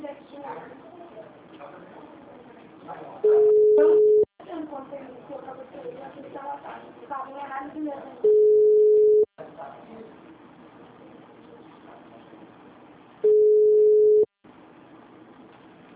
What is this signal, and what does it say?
Ringback tone (Europe, 425 Hz), with speech.